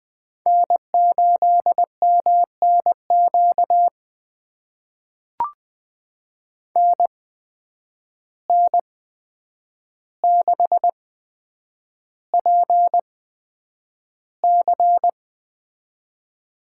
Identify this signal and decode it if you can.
Morse code, heard as a keyed tone, about 20 wpm, spaced as for 9 wpm, 700 Hz. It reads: N8MNQ NN6PC
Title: USA Call Signs - 5 Characters - 8x Character Spacing - 20wpm